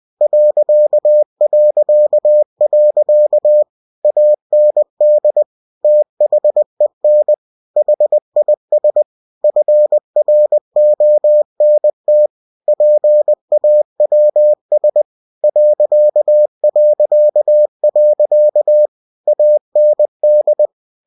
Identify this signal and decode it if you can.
Morse code, heard as a keyed tone, about 20 wpm, 600 Hz. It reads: ... AND THEN HIS FRONT PAWS ... AND
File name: cw-books / winnie-the-pooh / winnie-20wpm-page-0016